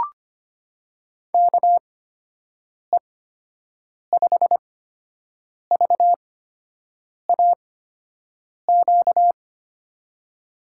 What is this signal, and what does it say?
Morse code, heard as a keyed tone, about 25 wpm, spaced as for 7 wpm, 700 Hz. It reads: KE5VAQ
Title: USA Call Signs - 6 Characters - 8x Character Spacing - 25wpm